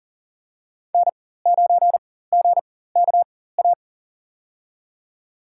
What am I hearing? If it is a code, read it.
Morse code, heard as a keyed tone, about 37 wpm, spaced as for 18 wpm, 700 Hz. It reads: N9GKA
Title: USA Call Signs - 5 Characters - 4x Character Spacing - 40wpm